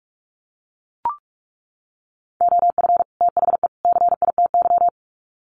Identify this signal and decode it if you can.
Morse code, heard as a keyed tone, about 44 wpm, 700 Hz. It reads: OF THE CITY
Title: Sets of 3 Words 45wpm